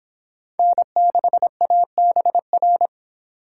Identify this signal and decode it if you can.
Morse code, heard as a keyed tone, about 30 wpm, 700 Hz. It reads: N6ABR